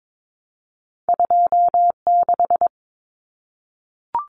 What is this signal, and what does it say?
Morse code, heard as a keyed tone, about 22 wpm, 700 Hz. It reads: 26
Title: Numbers - 2-Digit 22wpm